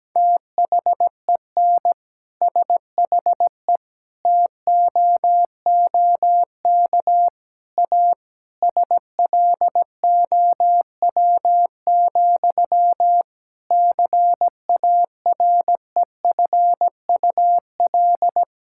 Morse code, heard as a keyed tone, about 17 wpm, 700 Hz. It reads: THEN SHE TOOK A SLOW, CAREFUL
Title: Sentences from Top 1600 Words 17wpm